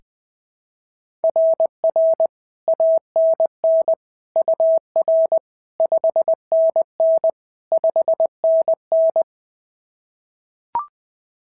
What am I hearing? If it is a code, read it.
Morse code, heard as a keyed tone, about 20 wpm, 650 Hz. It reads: RR ANN UR 5NN 5NN
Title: SOTA QSO Practice 20wpm